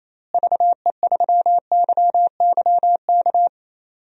Morse code, heard as a keyed tone, about 28 wpm, 700 Hz. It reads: VE3YYK
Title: DXCC-100 Call Signs - 6 Characters - Repeated 3x - 28wpm